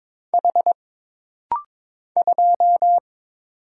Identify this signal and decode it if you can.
Morse code, heard as a keyed tone, about 22 wpm, 700 Hz. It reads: H 2
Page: This rapid-fire timing is geared to push your ICR (Instant Character Recognition) skills.